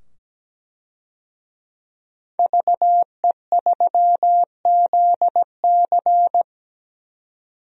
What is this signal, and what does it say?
Morse code, heard as a keyed tone, about 17 wpm, 700 Hz. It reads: VE3ZC
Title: DXCC-100 Call Signs - 5 Characters - 17wpm